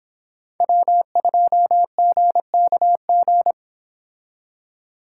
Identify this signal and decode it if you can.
Morse code, heard as a keyed tone, about 26 wpm, 700 Hz. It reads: W2GKG